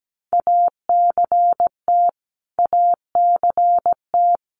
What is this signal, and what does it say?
Morse code, heard as a keyed tone, about 17 wpm, 700 Hz. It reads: ACT ACT
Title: Top 500 Words Repeated 17wpm